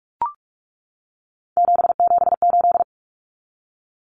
Morse code, heard as a keyed tone, about 44 wpm, 700 Hz. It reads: 778
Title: Numbers - 3-Digit 45wpm